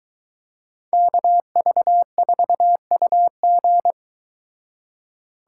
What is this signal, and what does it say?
Morse code, heard as a keyed tone, about 23 wpm, 700 Hz. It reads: KV4UG